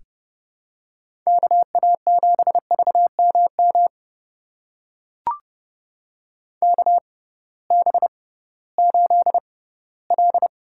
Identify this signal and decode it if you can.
Morse code, heard as a keyed tone, about 30 wpm, spaced as for 18 wpm, 700 Hz. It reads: KA7VMM KB8L
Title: USA Call Signs - 6 Characters - 6x Character Spacing - 30wpm